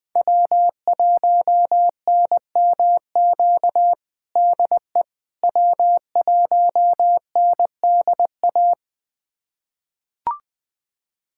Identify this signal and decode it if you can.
Morse code, heard as a keyed tone, about 19 wpm, 700 Hz. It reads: W1NMQ DE W1NDA
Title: USA Callsign DE Callsign 20wpm